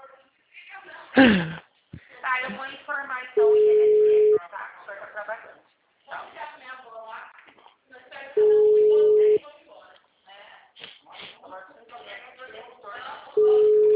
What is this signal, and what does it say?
Ringback tone (Europe, 425 Hz), with speech.
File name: out- -2001- - - .86852